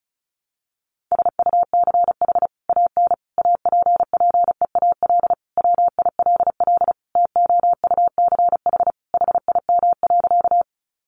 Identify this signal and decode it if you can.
Morse code, heard as a keyed tone, about 34 wpm, 700 Hz. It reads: SUCH AN APPEAL WILL TOUCH HIM.
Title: Sentences from Top 1300 Words 35wpm